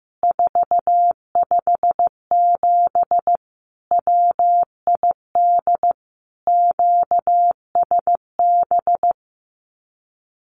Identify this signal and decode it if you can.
Morse code, heard as a keyed tone, about 15 wpm, 700 Hz. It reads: 457 WID QSB
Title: Common QSO Phrases 15wpm